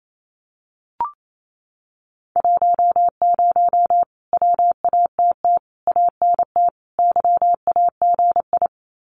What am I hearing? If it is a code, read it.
Morse code, heard as a keyed tone, about 28 wpm, 700 Hz. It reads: 10 WATT ANT YAGI
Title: Common QSO Phrases Encore 28wpm